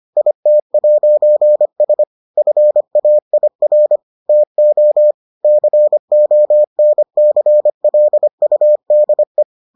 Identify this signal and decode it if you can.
Morse code, heard as a keyed tone, about 25 wpm, 600 Hz. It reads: IT'S FAIR TO CONCLUDE